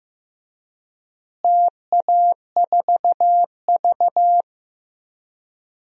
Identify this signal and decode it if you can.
Morse code, heard as a keyed tone, about 15 wpm, 700 Hz. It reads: TA4V
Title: DXCC-100 Call Signs - 4 Characters - Repeated 3x - 15wpm